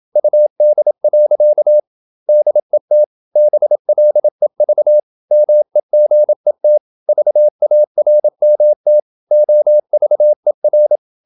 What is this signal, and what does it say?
Morse code, heard as a keyed tone, about 27 wpm, 600 Hz. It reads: UD. DET BLEV MEGET VARMT OVER
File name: Ferie QRQ 27wpm